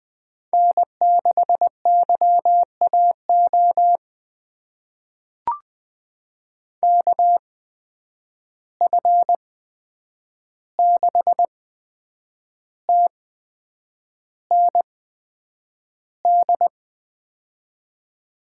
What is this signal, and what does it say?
Morse code, heard as a keyed tone, about 20 wpm, spaced as for 8 wpm, 700 Hz. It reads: N6YAO KF6TND
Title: USA Call Signs - 8x Character Spacing - 20wpm